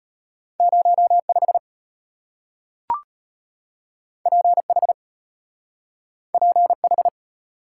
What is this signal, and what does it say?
Morse code, heard as a keyed tone, about 34 wpm, 700 Hz. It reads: 05 PH PH